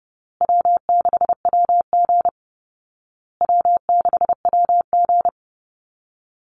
Morse code, heard as a keyed tone, about 30 wpm, 700 Hz. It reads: W6WG W6WG
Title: Call Signs Repeated 2x - 30wpm